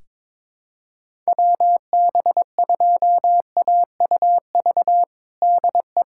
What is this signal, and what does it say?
Morse code, heard as a keyed tone, about 22 wpm, 700 Hz. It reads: WB2AUV DE
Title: USA Callsign DE Callsign - 6 Characters - 22wpm